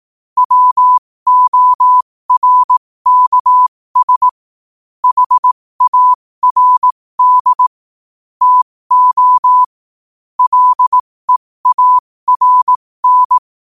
Morse code, heard as a keyed tone, about 18 wpm, spaced as for 15 wpm, 1000 Hz. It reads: WORKS HARD TO LEARN
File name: qotdfiles / 15wpm